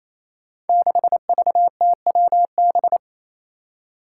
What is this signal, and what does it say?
Morse code, heard as a keyed tone, about 28 wpm, 700 Hz. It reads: 6VTWB
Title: Five Challenging Letter-Number Combinations 28wpm